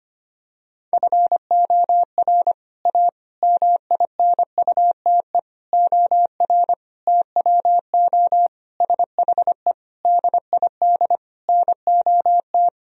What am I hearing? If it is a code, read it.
Morse code, heard as a keyed tone, about 25 wpm, 700 Hz. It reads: FOR A MINUTE OR TWO SHE DID NOT
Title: Sentences from Top 1000 Words - Review Part 1 - 25wpm